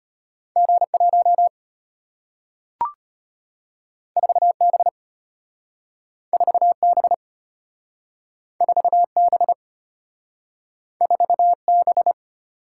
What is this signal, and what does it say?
Morse code, heard as a keyed tone, about 35 wpm, 700 Hz. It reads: G1 4B 4B 4B 4B